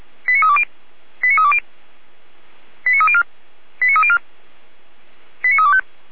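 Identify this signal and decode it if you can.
Five-tone selective call (ZVEI), heard as quick runs of five tones, tones of about 80 ms. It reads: 89320 89320 89304 89304 89325
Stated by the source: ZVEI five-tone selective call received on an SDR.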